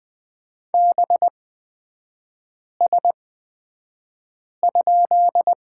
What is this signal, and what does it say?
Morse code, heard as a keyed tone, about 20 wpm, spaced as for 5 wpm, 690 Hz. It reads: BS?
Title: 229 Lesson 37e - Letter x - Three Characters 20wpm